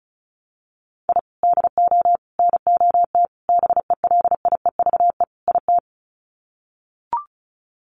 Morse code, heard as a keyed tone, about 34 wpm, 700 Hz. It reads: I DO NOT BELIEVE IT
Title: Sets of 5 Words 35wpm